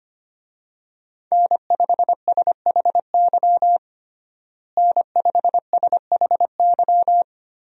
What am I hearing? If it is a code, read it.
Morse code, heard as a keyed tone, about 25 wpm, 700 Hz. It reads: N5SHY N5SHY
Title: USA Call Signs - 5 Characters - Repeated 2x - 25wpm